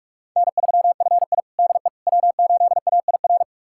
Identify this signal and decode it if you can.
Morse code, heard as a keyed tone, about 42 wpm, 700 Hz. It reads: N3FI DE W8AIR
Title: USA Callsign DE Callsign 45wpm